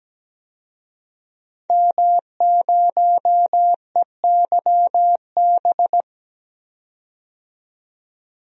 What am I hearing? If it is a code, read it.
Morse code, heard as a keyed tone, about 17 wpm, 700 Hz. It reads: M0EYB
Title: DXCC-100 Call Signs - 5 Characters - 17wpm